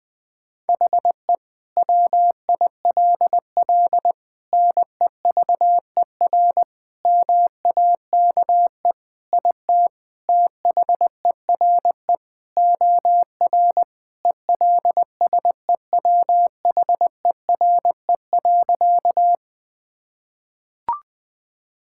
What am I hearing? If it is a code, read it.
Morse code, heard as a keyed tone, about 21 wpm, 700 Hz. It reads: HE WILL NEVER MAKE IT THERE OR ELSEWHERE.
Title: Sentences from Top 1600 Words 20wpm